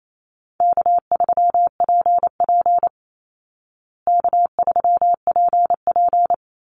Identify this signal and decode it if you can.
Morse code, heard as a keyed tone, about 28 wpm, 700 Hz. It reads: K3PP K3PP
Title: Easy Call Signs Repeated 2x - 28wpm